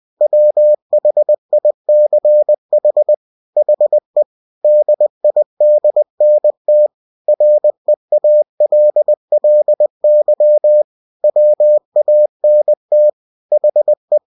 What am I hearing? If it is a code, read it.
Morse code, heard as a keyed tone, about 20 wpm, 600 Hz. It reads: WHICH HE DIDNT REALLY WANT HE
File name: cw-books / winnie-the-pooh / winnie-20wpm-page-0100